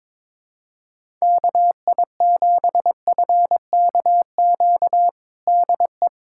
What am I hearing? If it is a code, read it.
Morse code, heard as a keyed tone, about 22 wpm, 700 Hz. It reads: KI7FKQ DE
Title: USA Callsign DE Callsign - 6 Characters - 22wpm